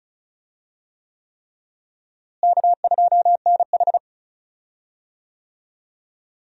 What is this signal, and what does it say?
Morse code, heard as a keyed tone, about 34 wpm, 700 Hz. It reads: K2NH
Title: USA Call Signs - 4 Characters - Repeated 3x - 35wpm